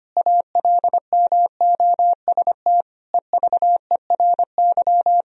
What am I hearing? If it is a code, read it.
Morse code, heard as a keyed tone, about 25 wpm, 700 Hz. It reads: ALMOST EVERY
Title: Sentences from Top 1000 Words - Review Part 4 - 25wpm